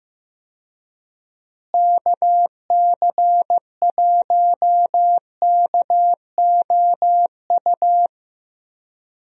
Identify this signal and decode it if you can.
Morse code, heard as a keyed tone, about 15 wpm, 700 Hz. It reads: KC1KOU
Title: USA Call Signs - 6 Characters - Repeated 3x - 15wpm